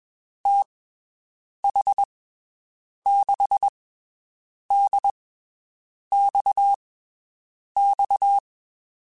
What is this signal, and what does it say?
Morse code, heard as a keyed tone, about 22 wpm, spaced as for 7 wpm, 800 Hz. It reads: TH6DXX
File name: (spaced) / QSOs Spaced-mp3s / WD0BGP 1 @20 sp